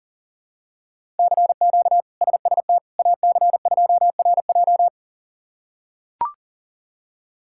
Morse code, heard as a keyed tone, about 40 wpm, 700 Hz. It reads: CQ SST AC2RJ
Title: K1USN SST QSO Practice - 40wpm